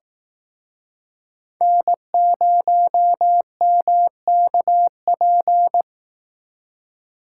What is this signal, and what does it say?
Morse code, heard as a keyed tone, about 18 wpm, 700 Hz. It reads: N0MKP
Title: Speed Racing - USA 1x3 Call Signs 15wpm